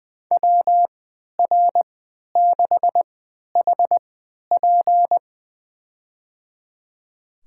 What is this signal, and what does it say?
Morse code, heard as a keyed tone, about 20 wpm, spaced as for 11 wpm, 700 Hz. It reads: WR6HP
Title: USA Call Signs - 5 Characters - 3x Character Spacing - 20wpm